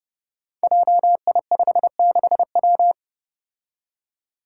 Morse code, heard as a keyed tone, about 32 wpm, 700 Hz. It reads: JI56W
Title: Five Challenging Letter-Number Combinations 30wpm